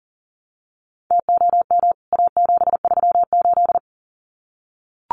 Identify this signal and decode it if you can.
Morse code, heard as a keyed tone, about 40 wpm, 700 Hz. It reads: TOM A738
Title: CWT Exchange 40wpm